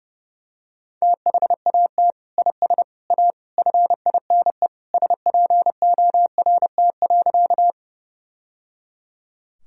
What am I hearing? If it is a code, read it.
Morse code, heard as a keyed tone, about 30 wpm, 700 Hz. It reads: THAT IS A FINE SPORT.
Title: Sentences from Top 1000 Words - Review Part 4 - 30wpm